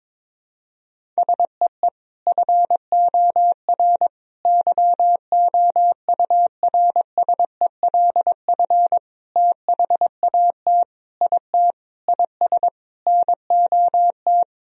Morse code, heard as a keyed tone, about 22 wpm, 700 Hz. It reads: SEE FOR YOURSELF THAT IT IS NOT
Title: Sentences from Top 1000 Words - Review Part 1 - 22wpm